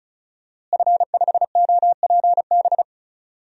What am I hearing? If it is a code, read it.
Morse code, heard as a keyed tone, about 36 wpm, 700 Hz. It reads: F5OPB
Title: DXCC-100 Call Signs - 5 Characters - Repeated 2x - 35wpm